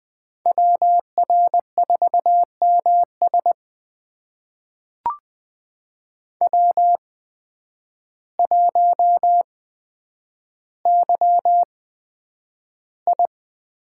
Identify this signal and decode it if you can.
Morse code, heard as a keyed tone, about 20 wpm, spaced as for 9 wpm, 700 Hz. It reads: WR4MS W1YI
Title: USA Call Signs - 5 Characters - 8x Character Spacing - 20wpm